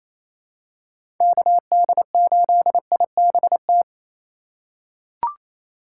Morse code, heard as a keyed tone, about 27 wpm, 700 Hz. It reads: KD8IBT
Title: USA Call Signs - 4x Character Spacing - 28wpm